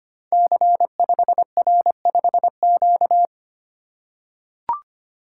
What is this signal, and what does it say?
Morse code, heard as a keyed tone, about 25 wpm, 700 Hz. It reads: C5R5Q
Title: Code Groups - 5 Letter:Numbers - 25wpm